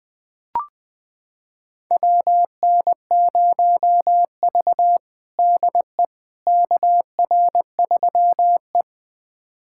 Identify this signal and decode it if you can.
Morse code, heard as a keyed tone, about 20 wpm, 700 Hz. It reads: WN0V DE KR3E
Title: USA Callsign DE Callsign - 4 Characters - 20wpm